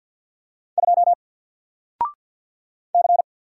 Morse code, heard as a keyed tone, about 54 wpm, 700 Hz. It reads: 2 C
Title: Single Character (Letter:Number:Punctuation:Prosign) 50wpm